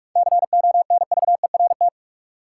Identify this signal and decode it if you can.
Morse code, heard as a keyed tone, about 46 wpm, 700 Hz. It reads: CONVERT
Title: Seven Letter Words - 45wpm